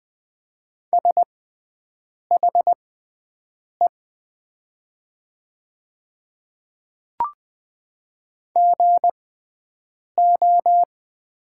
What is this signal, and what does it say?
Morse code, heard as a keyed tone, about 20 wpm, spaced as for 7 wpm, 700 Hz. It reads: SHE GO
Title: 6x Character Spacing - Top 100 Words - 20wpm